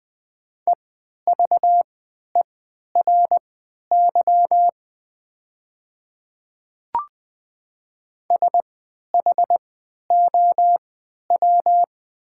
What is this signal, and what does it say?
Morse code, heard as a keyed tone, about 20 wpm, spaced as for 11 wpm, 700 Hz. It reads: EVERY SHOW
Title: 3x Character Spacing - Top 200 Words - 20wpm